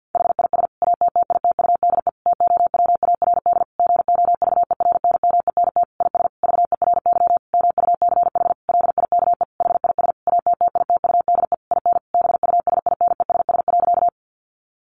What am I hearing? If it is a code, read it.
Morse code, heard as a keyed tone, about 50 wpm, 700 Hz. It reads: HIS ATTITUDE TOWARD GOVERNMENT IS VERY MUCH LIKE HIS ATTITUDE IN BUSINESS.
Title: Sentences from Top 1300 Words 50wpm